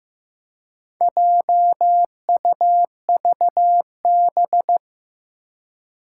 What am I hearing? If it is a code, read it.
Morse code, heard as a keyed tone, about 15 wpm, 700 Hz. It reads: JUVB
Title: Four Challenging Letter-Number Combinations 15wpm